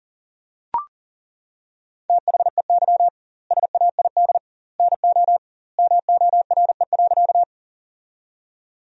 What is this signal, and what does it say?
Morse code, heard as a keyed tone, about 40 wpm, 700 Hz. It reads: THEY SAID NO MORE.
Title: 2x Word-Spacing - Sentences from Top 100 Words - 40wpm